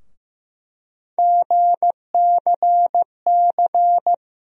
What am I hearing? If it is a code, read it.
Morse code, heard as a keyed tone, about 16 wpm, 700 Hz. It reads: GCC
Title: USA Call Sign Postfixes - 3 Letters 15wpm